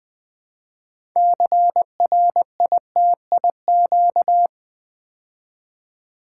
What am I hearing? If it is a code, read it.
Morse code, heard as a keyed tone, about 20 wpm, 700 Hz. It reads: CRITIQ
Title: Build A Word - 2 Letters At A Time - 8-Letter Words - 20wpm